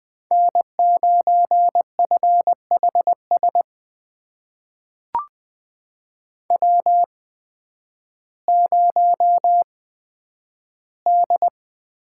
Morse code, heard as a keyed tone, about 20 wpm, spaced as for 11 wpm, 700 Hz. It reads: N9FHS W0D
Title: USA Call Signs - 5 Characters - 8x Character Spacing - 20wpm